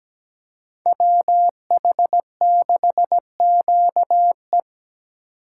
Morse code, heard as a keyed tone, about 20 wpm, 700 Hz. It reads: WH6QE
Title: Speed Racing - USA 2x2 Call Signs 17wpm